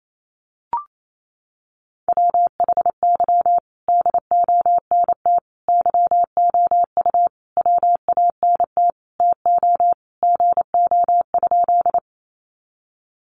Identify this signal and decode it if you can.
Morse code, heard as a keyed tone, about 28 wpm, 700 Hz. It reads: WHY DONT YOU WANT TO GO?
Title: Sentences from Top 300 Words 28wpm